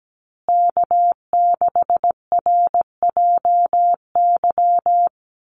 Morse code, heard as a keyed tone, about 17 wpm, 700 Hz. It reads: K6RJY
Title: USA Call Signs Repeated 2x 17wpm